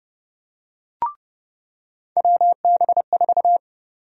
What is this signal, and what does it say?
Morse code, heard as a keyed tone, about 29 wpm, 700 Hz. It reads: WB4
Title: USA Call Sign Prefixes - 30wpm